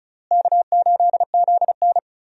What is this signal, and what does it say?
Morse code, heard as a keyed tone, about 34 wpm, 700 Hz. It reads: K8ZN
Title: USA Call Signs - 4 Characters - Repeated 3x - 35wpm